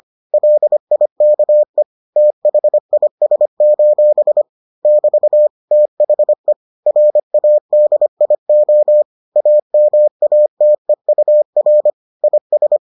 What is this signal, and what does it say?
Morse code, heard as a keyed tone, about 25 wpm, 600 Hz. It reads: LIKE THIS: <BT> THE RADIO AMATEUR IS